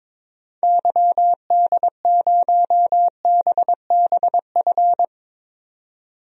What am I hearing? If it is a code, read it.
Morse code, heard as a keyed tone, about 22 wpm, 700 Hz. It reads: YD0BBF
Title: DXCC-100 Call Signs - 6 Characters - 22wpm